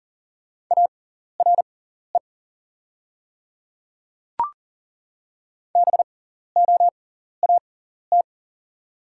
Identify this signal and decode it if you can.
Morse code, heard as a keyed tone, about 40 wpm, spaced as for 14 wpm, 700 Hz. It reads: ARE BOAT
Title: 6x Character Spacing - Top 500 Words - 40wpm